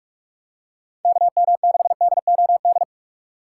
Morse code, heard as a keyed tone, about 46 wpm, 700 Hz. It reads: KM6DOD
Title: USA Call Signs - 6 Characters - 2x Character Spacing - 45wpm